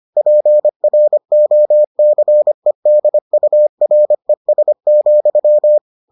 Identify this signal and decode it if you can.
Morse code, heard as a keyed tone, about 25 wpm, 600 Hz. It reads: PROCEDURES,